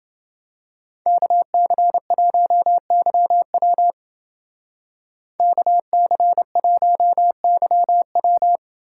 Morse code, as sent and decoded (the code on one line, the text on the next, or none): -.- -.-. .---- -.-- .-- / -.- -.-. .---- -.-- .--
KC1YW KC1YW